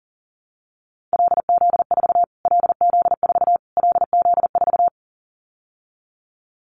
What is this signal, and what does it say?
Morse code, heard as a keyed tone, about 40 wpm, 700 Hz. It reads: LZ4 LZ4 LZ4
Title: Callsign Prefixes for Top 25 DXCC - Repeated 3x - 40wpm